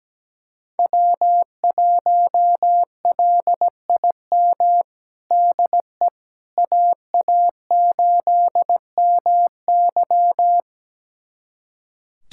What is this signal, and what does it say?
Morse code, heard as a keyed tone, about 17 wpm, 700 Hz. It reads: W1LIM DE AA8MY
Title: USA Callsign DE Callsign - 5 Characters - 17wpm